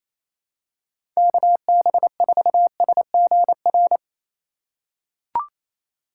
Morse code, heard as a keyed tone, about 27 wpm, 700 Hz. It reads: KB4SGR
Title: USA Call Signs - 3x Character Spacing - 28wpm